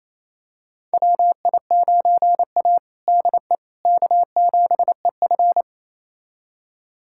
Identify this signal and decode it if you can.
Morse code, heard as a keyed tone, about 28 wpm, 700 Hz. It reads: WI9A DE K7EF
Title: USA Callsign DE Callsign 28wpm